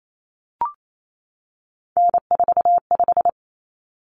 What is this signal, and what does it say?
Morse code, heard as a keyed tone, about 28 wpm, 700 Hz. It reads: N45
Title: Common Cut Numbers - 3-Digit 28wpm